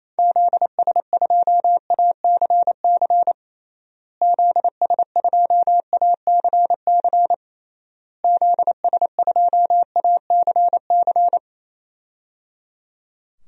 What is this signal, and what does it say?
Morse code, heard as a keyed tone, about 28 wpm, 700 Hz. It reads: ZS2ACC ZS2ACC ZS2ACC
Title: DXCC-100 Call Signs - 6 Characters - Repeated 3x - 28wpm